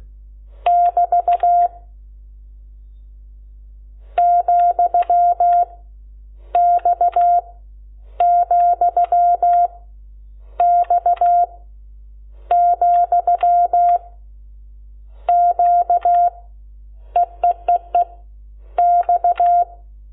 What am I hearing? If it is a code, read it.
Morse code, heard as a keyed tone, about 16 wpm, 700 Hz. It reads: <BT> , X , X , Q EEEE X
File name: Lecciones / CLASE Nº12